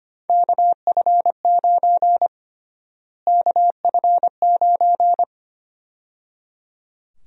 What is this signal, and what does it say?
Morse code, heard as a keyed tone, about 25 wpm, 700 Hz. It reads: KF9 KF9
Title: USA Call Sign Prefixes - Repeated 2x - 25wpm